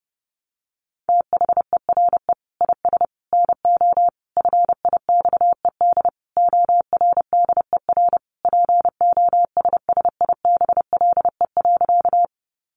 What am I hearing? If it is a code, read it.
Morse code, heard as a keyed tone, about 30 wpm, 700 Hz. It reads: THERE IS NO FIXED ORDER POSSIBLE.
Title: Sentences from Top 1100 Words 30wpm